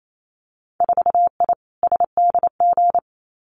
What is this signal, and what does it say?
Morse code, heard as a keyed tone, about 28 wpm, 700 Hz. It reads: 4I SDG
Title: Winter Field Day Exchanges Encore 28wpm